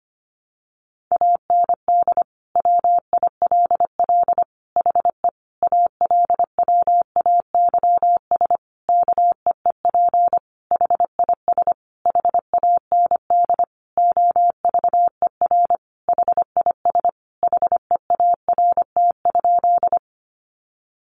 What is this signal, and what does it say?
Morse code, heard as a keyed tone, about 25 wpm, 700 Hz. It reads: AND WILL HE ALWAYS KEEP HIS HAND OVER HIS HEART?
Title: Sentences from Top 1200 Words 25wpm